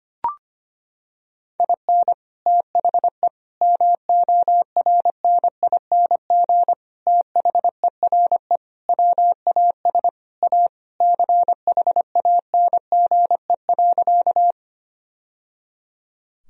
Morse code, heard as a keyed tone, about 25 wpm, 700 Hz. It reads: IN THE MORNING THERE WAS A CHANGE.
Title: Sentences from Top 1000 Words - Review Part 1 - 25wpm